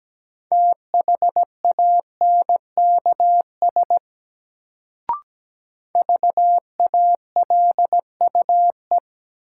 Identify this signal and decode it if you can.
Morse code, heard as a keyed tone, about 17 wpm, 700 Hz. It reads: THANKS VALUE
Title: Sentences from Top 600 Words - New Word Drill - 17wpm